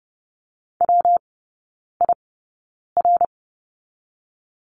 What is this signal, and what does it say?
Morse code, heard as a keyed tone, about 30 wpm, 700 Hz. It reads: W I R
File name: Lesson 19e - Letter w - Three Characters 30wpm